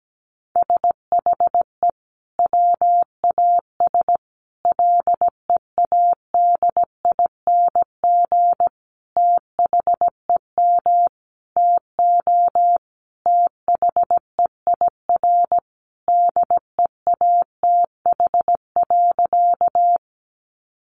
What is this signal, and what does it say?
Morse code, heard as a keyed tone, about 17 wpm, 700 Hz. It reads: SHE WAS LEADING THEM TO THEIR DEATH.
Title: Sentences from Top 1100 Words 17wpm